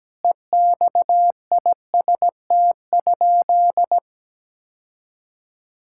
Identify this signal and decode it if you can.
Morse code, heard as a keyed tone, about 17 wpm, 700 Hz. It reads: EXIST?
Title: Sentences from Top 1000 Words - Review Part 5 - 17wpm